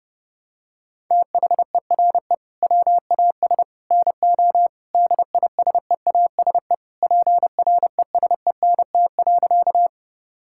Morse code, heard as a keyed tone, about 30 wpm, 700 Hz. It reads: THERE WAS NO DISEASE PRESENT.
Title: Sentences from Top 1000 Words - Review Part 6 - 30wpm